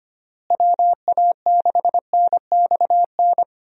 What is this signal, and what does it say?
Morse code, heard as a keyed tone, about 25 wpm, 700 Hz. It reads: WA6NXN
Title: USA Call Signs - 6 Characters - Repeated 2x - 25wpm